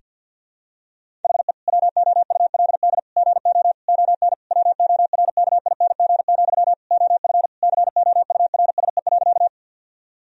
Morse code, heard as a keyed tone, about 50 wpm, 700 Hz. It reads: HE WOULD GO ON WORKING, OF COURSE.
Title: Sentences from Top 1000 Words - Review Part 4 - 50wpm